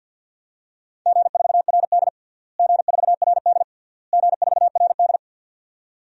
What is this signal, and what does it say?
Morse code, heard as a keyed tone, about 50 wpm, 700 Hz. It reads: G4RD G4RD G4RD
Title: DXCC-100 Call Signs - 4 Characters - Repeated 3x - 50wpm